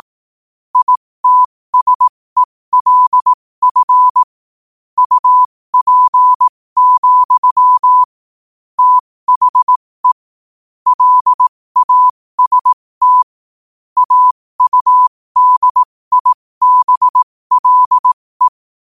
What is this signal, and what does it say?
Morse code, heard as a keyed tone, about 18 wpm, spaced as for 15 wpm, 1000 Hz. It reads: ITSELF UP, THE LAST AUDIBLE
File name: qotdfiles / 15wpm